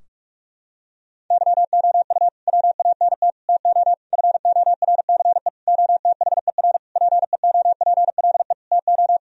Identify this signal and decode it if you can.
Morse code, heard as a keyed tone, about 47 wpm, 700 Hz. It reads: YOU WANT TO FORCE OTHER PEOPLE TO
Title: Sentences from Top 1000 Words - Review Part 3 - 45wpm